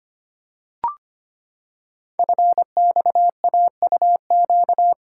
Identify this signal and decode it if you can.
Morse code, heard as a keyed tone, about 25 wpm, 700 Hz. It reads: FXAUQ
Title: Code Groups - 5 Letters - 25wpm